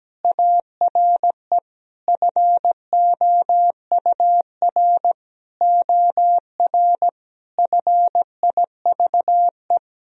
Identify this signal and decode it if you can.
Morse code, heard as a keyed tone, about 17 wpm, 700 Hz. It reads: ARE FOUR OR FIVE
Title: Sentences from Top 1000 Words - Review Part 2 - 17wpm